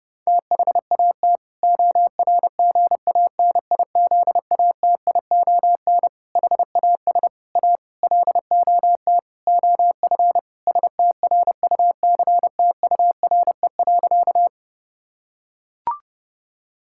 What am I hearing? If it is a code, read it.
Morse code, heard as a keyed tone, about 30 wpm, 700 Hz. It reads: THAT ORGANIZATION HAS A LOT OF STRUCTURE.
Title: Sentences from Top 1000 Words - Review Part 6 - 30wpm